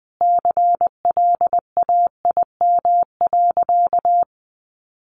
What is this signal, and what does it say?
Morse code, heard as a keyed tone, about 20 wpm, 700 Hz. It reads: CLAIM.
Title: Sentences from Top 1500 Words 20wpm